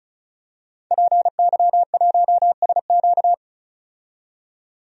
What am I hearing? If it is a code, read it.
Morse code, heard as a keyed tone, about 34 wpm, 700 Hz. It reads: PY1SQ
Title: DXCC-100 Call Signs - 5 Characters - 35wpm